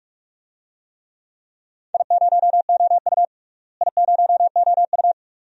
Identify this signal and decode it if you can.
Morse code, heard as a keyed tone, about 45 wpm, 700 Hz. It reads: I0OU I0OU
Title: DXCC-100 Call Signs - 4 Characters - Repeated 2x - 45wpm